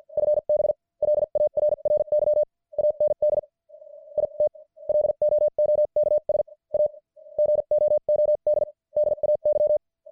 Morse code, heard as a keyed tone, about 50 wpm, spaced as for 42 wpm, 600 Hz. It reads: FB LARRY AND IT LOOKS A GOOD DAY
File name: archived / CWBros